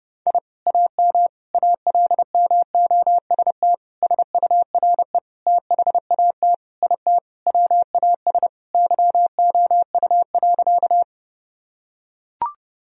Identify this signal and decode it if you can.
Morse code, heard as a keyed tone, about 30 wpm, 700 Hz. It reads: I AM ALMOST SURE THAT IT WAS YOU.
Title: Sentences from Top 1000 Words - Review Part 4 - 30wpm